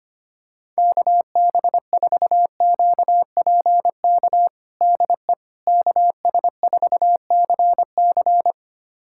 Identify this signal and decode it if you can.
Morse code, heard as a keyed tone, about 25 wpm, 700 Hz. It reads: KB4QPK DE KS4CC
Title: USA Callsign DE Callsign 25wpm